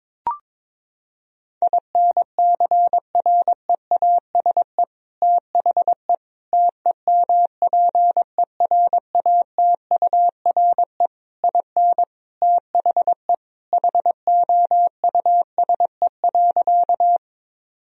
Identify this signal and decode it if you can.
Morse code, heard as a keyed tone, about 22 wpm, 700 Hz. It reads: INCREASE THE TEMPERATURE IN THE HOUSE.
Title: Sentences from Top 1000 Words - Review Part 4 - 22wpm